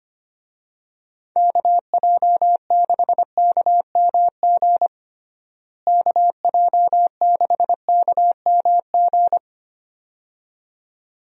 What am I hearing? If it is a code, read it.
Morse code, heard as a keyed tone, about 25 wpm, 700 Hz. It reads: KJ6KMG KJ6KMG
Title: USA Call Signs - 6 Characters - Repeated 2x - 25wpm